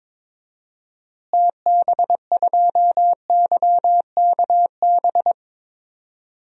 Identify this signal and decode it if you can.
Morse code, heard as a keyed tone, about 21 wpm, 700 Hz. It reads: TB2YKB
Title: DXCC-100 Call Signs - 6 Characters - Repeated 2x - 22wpm